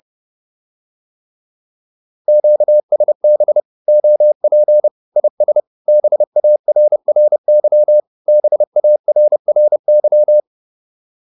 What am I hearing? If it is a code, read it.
Morse code, heard as a keyed tone, about 30 wpm, 600 Hz. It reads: QSB OP IS BARRY BARRY